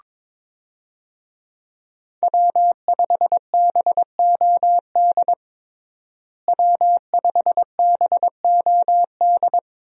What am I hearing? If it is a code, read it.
Morse code, heard as a keyed tone, about 22 wpm, 690 Hz. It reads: W5BOD W5BOD
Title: USA Call Signs - 5 Characters - Repeated 3x - 22wpm